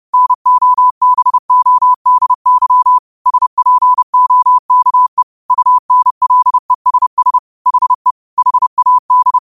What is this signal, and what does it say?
Morse code, heard as a keyed tone, about 30 wpm, 1000 Hz. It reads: NOBODY SPOKE UNLESS HE HAD
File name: qotdfiles / 30wpm